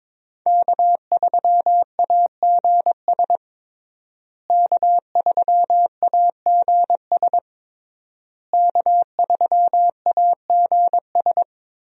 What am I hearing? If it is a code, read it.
Morse code, heard as a keyed tone, about 22 wpm, 700 Hz. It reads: K3AGS K3AGS K3AGS
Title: USA Call Signs - 5 Characters - Repeated 3x - 22wpm